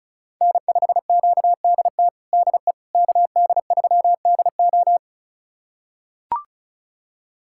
Morse code, heard as a keyed tone, about 34 wpm, 700 Hz. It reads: N5QDT DE KD3DO
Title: USA Callsign DE Callsign - 5 Characters - 35wpm